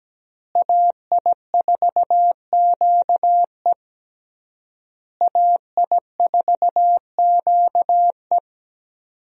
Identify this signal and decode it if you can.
Morse code, heard as a keyed tone, about 17 wpm, 700 Hz. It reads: AI4QE AI4QE
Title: USA Call Signs - 5 Characters - Repeated 3x - 17wpm